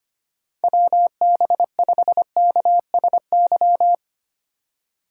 Morse code, heard as a keyed tone, about 25 wpm, 700 Hz. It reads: WB5KSY
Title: USA Call Signs - 6 Characters - 4x Character Spacing - 25wpm